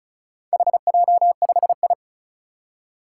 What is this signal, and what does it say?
Morse code, heard as a keyed tone, about 36 wpm, 700 Hz. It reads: HJ5I
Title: Four Challenging Letter-Number Combinations 35wpm